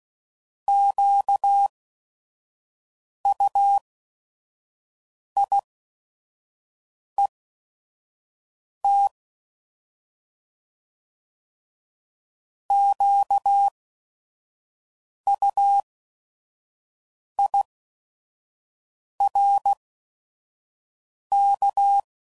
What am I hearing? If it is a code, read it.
Morse code, heard as a keyed tone, about 16 wpm, spaced as for 5 wpm, 800 Hz. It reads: QUIET QUIRK
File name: Prefixes / By First Letter / Qs